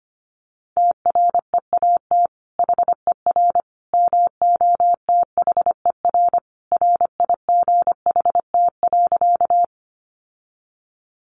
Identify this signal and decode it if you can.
Morse code, heard as a keyed tone, about 25 wpm, 700 Hz. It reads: TREAT HER MOTHER RIGHT.
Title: Sentences from Top 1300 Words 25wpm